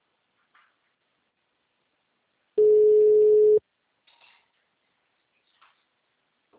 Ringback tone (Europe, 425 Hz).